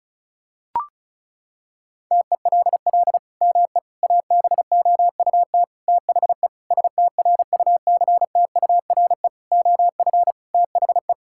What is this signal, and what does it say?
Morse code, heard as a keyed tone, about 34 wpm, 700 Hz. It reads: TELL ME ABOUT THE STRUCTURE OF THE
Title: Sentences from Top 1000 Words - Review Part 6 - 35wpm